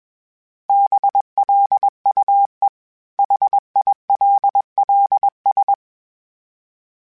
Morse code, heard as a keyed tone, about 21 wpm, 800 Hz. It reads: BLUE HILLS
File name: fromSMs / FromSMs mp3s / Cities@20